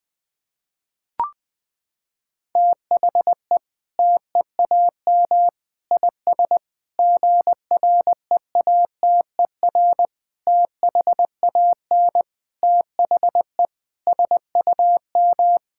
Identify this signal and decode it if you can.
Morse code, heard as a keyed tone, about 21 wpm, 700 Hz. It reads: THE TEAM IS GREATER THAN THE SUM
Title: Sentences with 8-Word Phrases - 20wpm